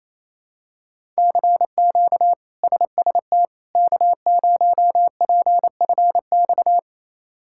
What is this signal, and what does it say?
Morse code, heard as a keyed tone, about 27 wpm, 700 Hz. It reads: CQ SST K0PFX
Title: K1USN SST QSO Practice - 28wpm